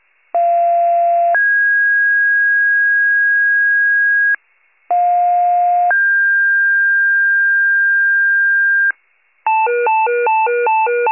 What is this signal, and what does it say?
Two-tone page, heard as a one-second tone, then a three-second tone, about 690 then 1650 Hz, about 710 then 1600 Hz. It is heard from 0.3 to 8.9 s.
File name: pager / 46 RESCUE DUTY / wcalert